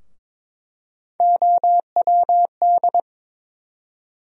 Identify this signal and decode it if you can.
Morse code, heard as a keyed tone, about 22 wpm, 700 Hz. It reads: OWD
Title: USA Call Sign Postfixes - 3 Letters 22wpm